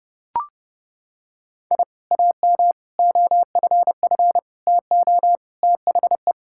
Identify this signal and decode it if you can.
Morse code, heard as a keyed tone, about 30 wpm, 700 Hz. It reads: I AM OFF TO THE
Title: Sentences from Top 1000 Words - Review Part 5 - 30wpm